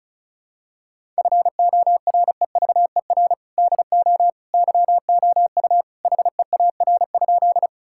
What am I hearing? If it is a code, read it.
Morse code, heard as a keyed tone, about 36 wpm, 700 Hz. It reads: FOREVER DO YOU HEAR?
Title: Sentences from Top 1600 Words 35wpm